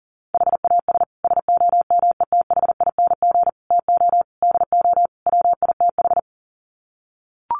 Morse code, heard as a keyed tone, about 40 wpm, 700 Hz. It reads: HAS SOMETHING TO DO WITH
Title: Sets of 5 Words Encore 40wpm